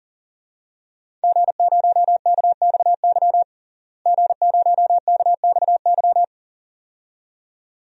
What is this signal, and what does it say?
Morse code, heard as a keyed tone, about 40 wpm, 700 Hz. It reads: G0KXY G0KXY
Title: DXCC-100 Call Signs - 5 Characters - Repeated 2x - 40wpm